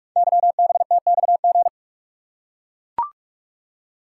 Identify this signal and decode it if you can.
Morse code, heard as a keyed tone, about 44 wpm, 700 Hz. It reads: YBTXG
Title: Code Groups - 5 Letters - 45wpm